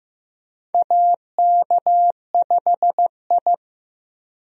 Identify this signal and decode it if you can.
Morse code, heard as a keyed tone, about 15 wpm, 700 Hz. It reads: AK5I
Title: USA Call Signs - 4 Characters - Repeated 3x - 15wpm